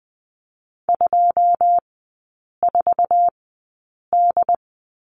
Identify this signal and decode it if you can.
Morse code, heard as a keyed tone, about 20 wpm, spaced as for 8 wpm, 700 Hz. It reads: 24D
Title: 100 Lesson 17e - Letter u - Three Characters 20wpm